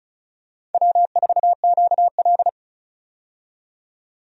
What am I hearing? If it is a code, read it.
Morse code, heard as a keyed tone, about 34 wpm, 700 Hz. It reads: W4QL
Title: USA Call Signs - 4 Characters - 35wpm